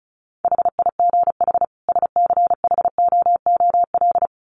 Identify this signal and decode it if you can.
Morse code, heard as a keyed tone, about 34 wpm, 700 Hz. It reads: HIGH SCHOOL
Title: Sets of 2 Words 35wpm